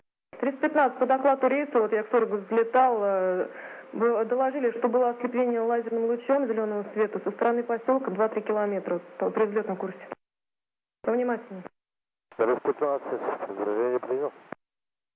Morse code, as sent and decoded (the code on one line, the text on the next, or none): none